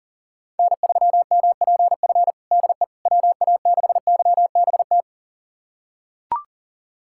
Morse code, heard as a keyed tone, about 40 wpm, 700 Hz. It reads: N3MPF DE WA6YBT
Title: USA Callsign DE Callsign 40wpm